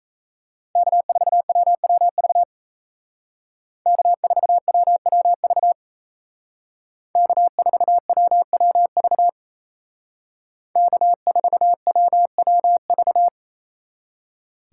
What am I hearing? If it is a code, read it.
Morse code, heard as a keyed tone, about 36 wpm, 700 Hz. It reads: K4WWV K4WWV K4WWV K4WWV